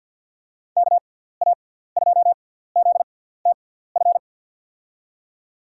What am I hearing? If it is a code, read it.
Morse code, heard as a keyed tone, about 52 wpm, spaced as for 17 wpm, 700 Hz. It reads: KA2ZTF
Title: USA Call Signs - 6x Character Spacing - 50wpm